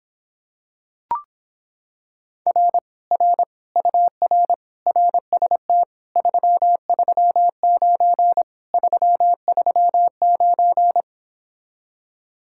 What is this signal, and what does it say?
Morse code, heard as a keyed tone, about 26 wpm, 700 Hz. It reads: R R UR RST 339 339